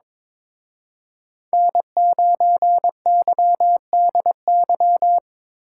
Morse code, heard as a keyed tone, about 25 wpm, 700 Hz. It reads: N9YDY
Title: Speed Racing - USA 1x3 Call Signs 22wpm